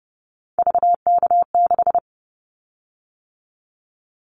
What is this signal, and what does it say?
Morse code, heard as a keyed tone, about 30 wpm, 700 Hz. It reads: VK6
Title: Callsign Prefixes for Top 50 DXCC - 30wpm